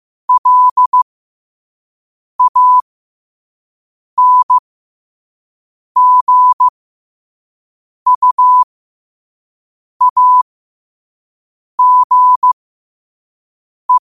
Morse code, heard as a keyed tone, about 15 wpm, spaced as for 5 wpm, 1000 Hz. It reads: LANGUAGE
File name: qotdfiles / 5wpm